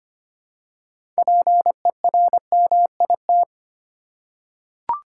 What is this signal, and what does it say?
Morse code, heard as a keyed tone, about 25 wpm, 700 Hz. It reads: PERMIT
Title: Six Letter Words - 25wpm